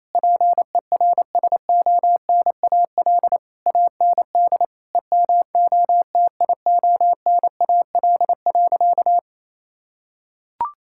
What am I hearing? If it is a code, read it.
Morse code, heard as a keyed tone, about 28 wpm, 700 Hz. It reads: PERSONAL AND EMOTIONAL.
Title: Sentences from Top 1000 Words - Review Part 5 - 28wpm